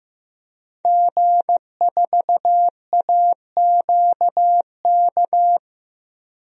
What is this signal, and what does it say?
Morse code, heard as a keyed tone, about 15 wpm, 700 Hz. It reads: G4AQK
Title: DXCC-100 Call Signs - 5 Characters - 15wpm